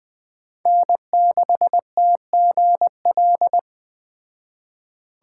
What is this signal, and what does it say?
Morse code, heard as a keyed tone, about 20 wpm, 700 Hz. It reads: N6TGL
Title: Speed Racing - USA Call Signs 17wpm